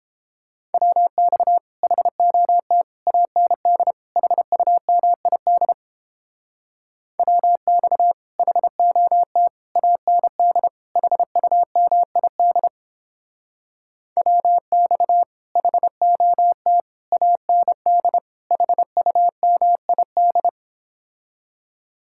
Morse code, heard as a keyed tone, about 31 wpm, 700 Hz. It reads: WX HOT AND HUMID WX HOT AND HUMID WX HOT AND HUMID